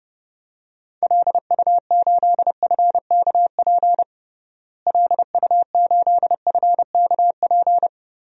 Morse code, heard as a keyed tone, about 30 wpm, 700 Hz. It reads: LU8FKP LU8FKP
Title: DXCC-100 Call Signs - 6 Characters - Repeated 3x - 30wpm